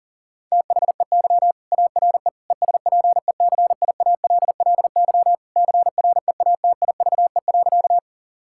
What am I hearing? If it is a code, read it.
Morse code, heard as a keyed tone, about 40 wpm, 700 Hz. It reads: THEY ARE ESPECIALLY CREATIVE.
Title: Sentences from Top 1000 Words - Review Part 5 - 40wpm